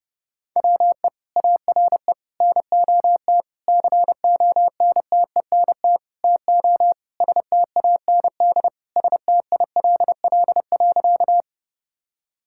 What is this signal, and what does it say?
Morse code, heard as a keyed tone, about 32 wpm, 700 Hz. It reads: WE ARE NOT CONTENT TO STAND STILL.
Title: Sentences from Top 1000 Words - Review Part 6 - 30wpm